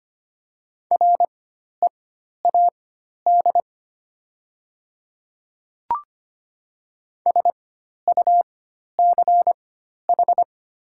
Morse code, heard as a keyed tone, about 25 wpm, spaced as for 12 wpm, 700 Hz. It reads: READ SUCH
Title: 4x Character Spacing - Top 200 Words - 25wpm